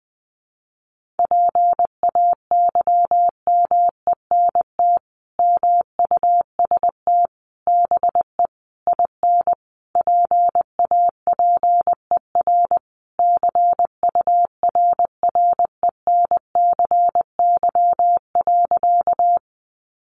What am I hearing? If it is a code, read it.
Morse code, heard as a keyed tone, about 20 wpm, 700 Hz. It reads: PAYMENT MUST BE IN PAPER CURRENCY.
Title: Sentences from Top 1500 Words 20wpm